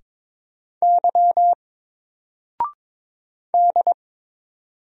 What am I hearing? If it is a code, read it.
Morse code, heard as a keyed tone, about 22 wpm, 700 Hz. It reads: Y D
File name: Single-Letter-Number-22wpm